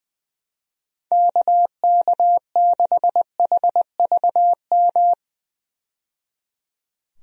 Morse code, heard as a keyed tone, about 21 wpm, 700 Hz. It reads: KK6HVM
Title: USA Call Signs - 6 Characters - 20wpm